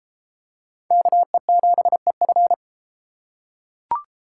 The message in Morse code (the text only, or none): KE7EF